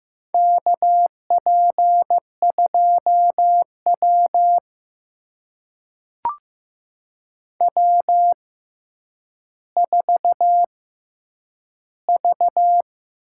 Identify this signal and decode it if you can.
Morse code, heard as a keyed tone, about 15 wpm, spaced as for 9 wpm, 700 Hz. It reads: KP2W W4V
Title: USA Call Signs - 4 Characters - 6x Character Spacing - 15wpm